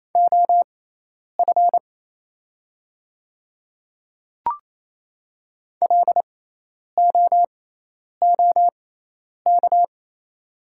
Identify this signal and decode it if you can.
Morse code, heard as a keyed tone, about 27 wpm, spaced as for 10 wpm, 700 Hz. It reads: OF LOOK
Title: 6x Character Spacing - Top 100 Words - 28wpm